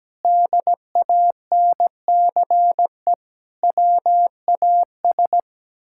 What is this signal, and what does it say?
Morse code, heard as a keyed tone, about 17 wpm, 700 Hz. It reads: DANCE WAS
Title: Sentences from Top 1000 Words - Review Part 6 - 17wpm